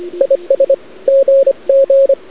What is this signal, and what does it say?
Morse code, heard as a keyed tone, about 25 wpm, 540 Hz, with radio noise. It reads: IS GG
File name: cd-odj / Suara / cwrag